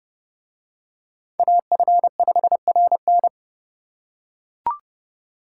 Morse code, heard as a keyed tone, about 27 wpm, 700 Hz. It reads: AF5RN